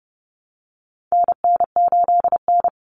Morse code, heard as a keyed tone, about 28 wpm, 700 Hz. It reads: NN8N
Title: Common Cut Numbers - 4-Digit 30wpm